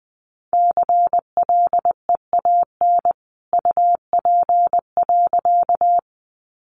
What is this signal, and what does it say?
Morse code, heard as a keyed tone, about 20 wpm, 700 Hz. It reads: CLEAN UP.
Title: A Sound Of Thunder 20wpm